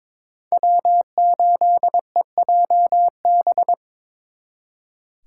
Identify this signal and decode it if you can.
Morse code, heard as a keyed tone, about 26 wpm, 700 Hz. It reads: W8EJB
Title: Speed Racing - USA 1x3 Call Signs 22wpm